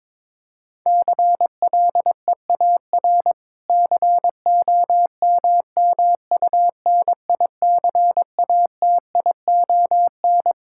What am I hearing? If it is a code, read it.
Morse code, heard as a keyed tone, about 23 wpm, 700 Hz. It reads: CLEAR COMMUNICATION
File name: Difficult Sentences with Hints - 22wpm